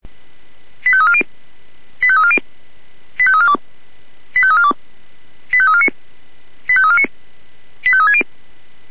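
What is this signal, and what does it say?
Five-tone selective call (ZVEI), heard as quick runs of five tones, tones of about 70 ms. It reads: 85340 85340 85342 85342 85349 85349 85360
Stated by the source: Five-tone ZVEI selective call received by SDR.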